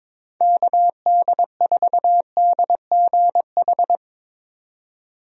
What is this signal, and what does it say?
Morse code, heard as a keyed tone, about 21 wpm, 700 Hz. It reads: KD4DGH
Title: USA Call Signs - 6 Characters - 22wpm